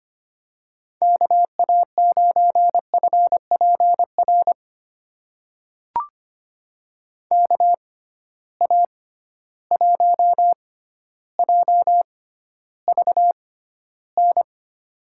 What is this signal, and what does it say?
Morse code, heard as a keyed tone, about 25 wpm, spaced as for 13 wpm, 700 Hz. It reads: KA9FPR KA1JVN
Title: USA Call Signs - 6 Characters - 6x Character Spacing - 25wpm